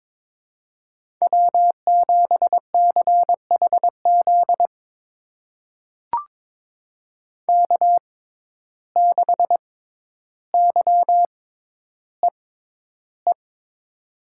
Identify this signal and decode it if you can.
Morse code, heard as a keyed tone, about 22 wpm, spaced as for 11 wpm, 700 Hz. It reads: W7CHZ K6YEE
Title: USA Call Signs - 5 Characters - 6x Character Spacing - 22wpm